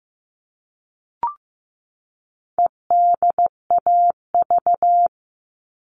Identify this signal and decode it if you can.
Morse code, heard as a keyed tone, about 15 wpm, 700 Hz. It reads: EDAV
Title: All Cut Numbers - 4-Digit 15wpm